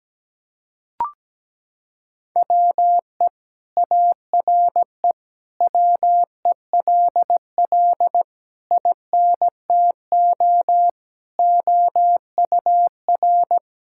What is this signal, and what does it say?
Morse code, heard as a keyed tone, about 17 wpm, 700 Hz. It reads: WE ARE WELL INTO OUR
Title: Sentences from Top 1000 Words - Review Part 5 - 17wpm